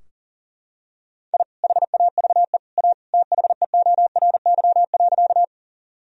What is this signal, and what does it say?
Morse code, heard as a keyed tone, about 40 wpm, 700 Hz. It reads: I HAVE A THEORY.
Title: Sentences from Top 1000 Words - Review Part 3 - 40wpm